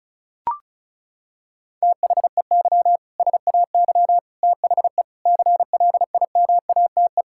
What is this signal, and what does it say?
Morse code, heard as a keyed tone, about 36 wpm, 700 Hz. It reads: THEY SAY THE CLIMATE
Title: Sentences from Top 1600 Words 35wpm